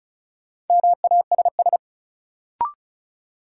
Morse code, heard as a keyed tone, about 35 wpm, 700 Hz. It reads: MASS
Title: Sets of 5 Words Encore - New Word Drill - 35wpm